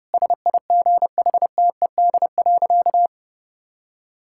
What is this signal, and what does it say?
Morse code, heard as a keyed tone, about 32 wpm, 700 Hz. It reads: SIGHTED.
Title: Pangram Challenge v2 - 30wpm